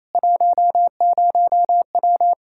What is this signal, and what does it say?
Morse code, heard as a keyed tone, about 28 wpm, 700 Hz. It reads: 10W
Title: Practice QSO - Exchange 2 - Info - 28wpm